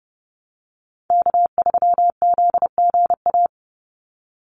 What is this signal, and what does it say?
Morse code, heard as a keyed tone, about 30 wpm, 700 Hz. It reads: K3ZGA
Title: Call Signs Repeated 3x - 30wpm